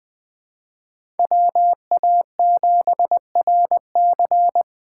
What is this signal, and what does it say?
Morse code, heard as a keyed tone, about 20 wpm, 700 Hz. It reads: WA7RC
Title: USA Call Signs - 2x Character Spacing - 20wpm